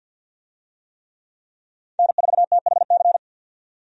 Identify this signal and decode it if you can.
Morse code, heard as a keyed tone, about 46 wpm, 700 Hz. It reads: N4THC
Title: USA Call Signs - 5 Characters - 50wpm